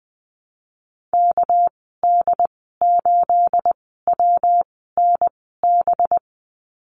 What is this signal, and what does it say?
Morse code, heard as a keyed tone, about 20 wpm, spaced as for 14 wpm, 700 Hz. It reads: KD8WNB
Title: 219 Lesson 35g - Number 0 - Call Signs 20wpm